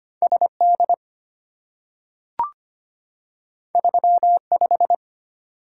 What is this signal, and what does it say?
Morse code, heard as a keyed tone, about 25 wpm, 700 Hz. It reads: SD 35
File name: Two Letter-Number Combinations - 25wpm